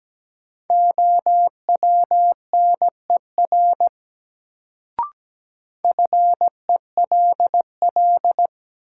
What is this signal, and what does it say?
Morse code, heard as a keyed tone, about 17 wpm, 700 Hz. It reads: OWNER FELL
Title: Sentences from Top 1600 Words - New Word Drill Alt - 17wpm